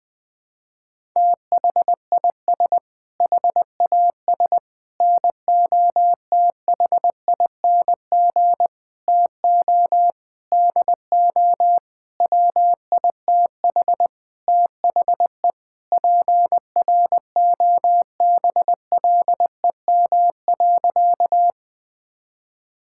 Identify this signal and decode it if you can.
Morse code, heard as a keyed tone, about 21 wpm, 700 Hz. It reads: THIS HAS NOTHING TO DO WITH THE PROBLEM.
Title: Sentences with 5-Word Phrases - 20wpm